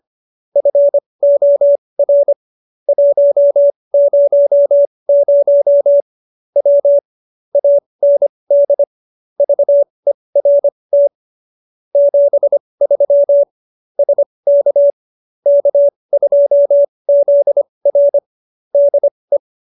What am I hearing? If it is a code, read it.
Morse code, heard as a keyed tone, about 25 wpm, spaced as for 20 wpm, 570 Hz. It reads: FOR 100 W AND VERT 73 SK K2ZR DE
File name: QSO302 20